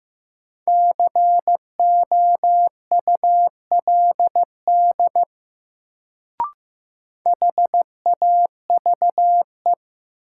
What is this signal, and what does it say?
Morse code, heard as a keyed tone, about 15 wpm, 700 Hz. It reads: COULD HAVE
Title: Sentences from Top 100 Words - New Word Drill - 15wpm